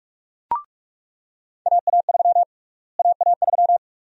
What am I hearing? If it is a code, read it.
Morse code, heard as a keyed tone, about 42 wpm, 700 Hz. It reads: AA3 AA3
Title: USA Call Sign Prefixes - Repeated 2x - 45wpm